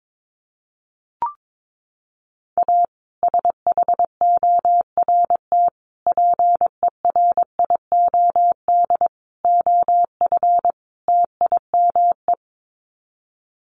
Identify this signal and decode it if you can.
Morse code, heard as a keyed tone, about 22 wpm, 700 Hz. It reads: A SHORT PERIOD OF TIME
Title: Sets of 5 Words Encore 22wpm